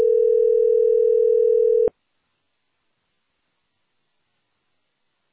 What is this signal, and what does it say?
Ringback tone (North America), 440 + 480 Hz.